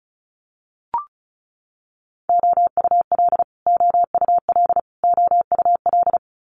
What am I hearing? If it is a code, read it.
Morse code, heard as a keyed tone, about 36 wpm, 700 Hz. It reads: OUL OUL OUL
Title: Multi-letter Phonemes Repeated 3x 35wpm